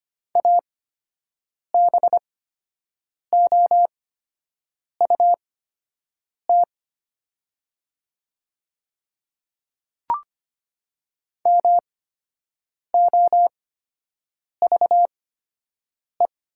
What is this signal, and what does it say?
Morse code, heard as a keyed tone, about 25 wpm, spaced as for 7 wpm, 700 Hz. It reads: ABOUT MOVE
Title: 8x Character Spacing - Top 300 Words - 25wpm